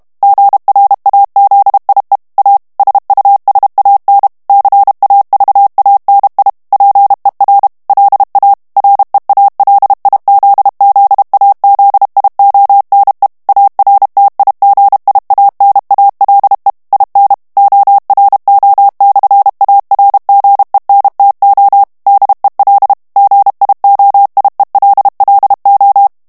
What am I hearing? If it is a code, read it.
Morse code, heard as a keyed tone, about 32 wpm, 800 Hz. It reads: GRAZIE A SUSAN CAVANI PER LA REALIZZAZIONE ARTIGIANALE IN ORO/ARGENTO DEL GIOIELLO
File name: morse mp3 / ITALIEN texte clair 1 vitesse 30 mots minute